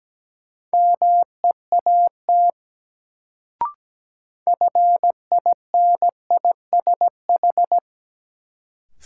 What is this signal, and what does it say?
Morse code, heard as a keyed tone, about 17 wpm, 700 Hz. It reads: MEAT FINISH
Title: Sentences from Top 1000 Words - Review Part 3 - New Word Drill - 17wpm